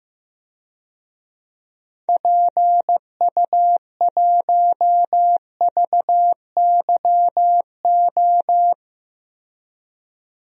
Morse code, heard as a keyed tone, about 15 wpm, 700 Hz. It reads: PU1VYO
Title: DXCC-100 Call Signs - 6 Characters - Repeated 2x - 15wpm